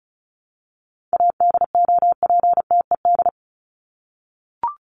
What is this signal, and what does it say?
Morse code, heard as a keyed tone, about 34 wpm, 700 Hz. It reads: ADOPTED
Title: ED Suffix - Top 100 Words with Base Words 35wpm